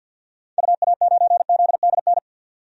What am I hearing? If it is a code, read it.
Morse code, heard as a keyed tone, about 51 wpm, 700 Hz. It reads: UA9ZDN
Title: DXCC-100 Call Signs - 6 Characters - 50wpm